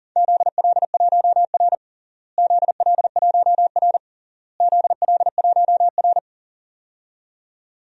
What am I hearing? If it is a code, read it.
Morse code, heard as a keyed tone, about 40 wpm, 700 Hz. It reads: ZL1R ZL1R ZL1R
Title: DXCC-100 Call Signs - 4 Characters - Repeated 3x - 40wpm